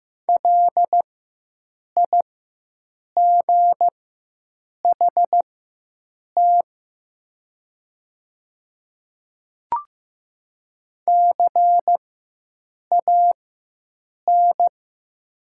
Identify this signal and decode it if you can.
Morse code, heard as a keyed tone, about 15 wpm, spaced as for 7 wpm, 700 Hz. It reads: LIGHT CAN
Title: 4x Character Spacing - Top 300 Words - 15wpm